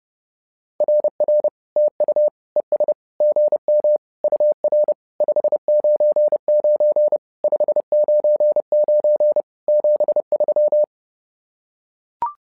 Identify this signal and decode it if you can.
Morse code, heard as a keyed tone, about 30 wpm, 600 Hz. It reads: RR TU ES GM UR 599 599 73
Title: SOTA QSO Practice 30wpm